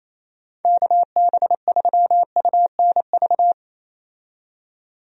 Morse code, heard as a keyed tone, about 28 wpm, 700 Hz. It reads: KB3UNV
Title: USA Call Signs - 6 Characters - 28wpm